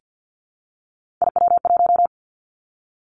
Morse code, heard as a keyed tone, about 50 wpm, 700 Hz. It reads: IW1
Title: Three Challenging Letter-Number Combinations 50wpm